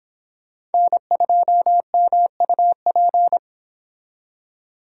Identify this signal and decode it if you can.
Morse code, heard as a keyed tone, about 31 wpm, 700 Hz. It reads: N2MUP